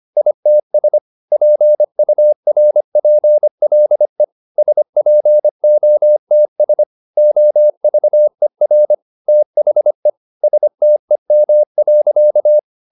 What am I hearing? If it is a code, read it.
Morse code, heard as a keyed tone, about 25 wpm, 600 Hz. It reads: ITS PURPLE SPOTS OVER THE STEM.